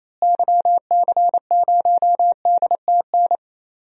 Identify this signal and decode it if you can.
Morse code, heard as a keyed tone, about 27 wpm, 700 Hz. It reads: YC0DTN
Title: DXCC-100 Call Signs - 6 Characters - Repeated 2x - 28wpm